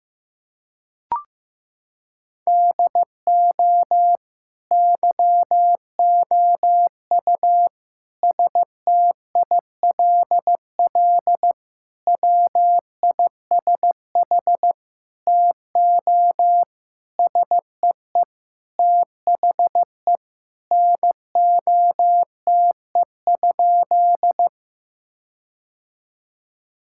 Morse code, heard as a keyed tone, about 15 wpm, 700 Hz. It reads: DO YOU STILL WISH TO SEE THE NOTE?
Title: Sentences from Top 1600 Words 15wpm